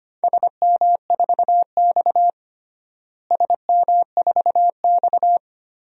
Morse code, heard as a keyed tone, about 25 wpm, 700 Hz. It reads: SM4X SM4X
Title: DXCC-100 Call Signs - 4 Characters - Repeated 3x - 25wpm